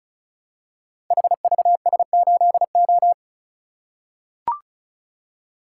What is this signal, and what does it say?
Morse code, heard as a keyed tone, about 36 wpm, 700 Hz. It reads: HVS8O
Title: Code Groups - 5 Letter:Numbers - 35wpm